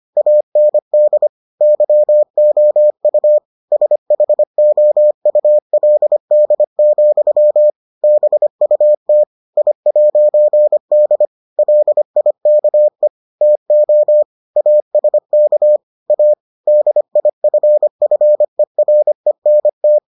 Morse code, heard as a keyed tone, about 25 wpm, 600 Hz. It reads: AND YOU SHOULD, BUT I'D LIKE TO ASK A DIFFERENT